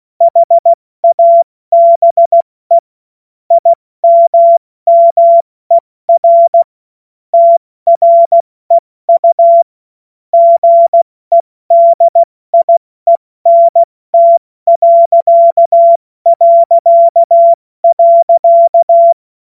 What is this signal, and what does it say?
Morse code, heard as a keyed tone, about 16 wpm, 680 Hz. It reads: HABE IMMER TREU GEDIENT...
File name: Witz-16-September-2025-70-F80